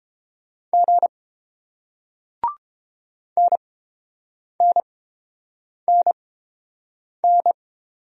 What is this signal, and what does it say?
Morse code, heard as a keyed tone, about 30 wpm, spaced as for 7 wpm, 700 Hz. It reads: G NNNN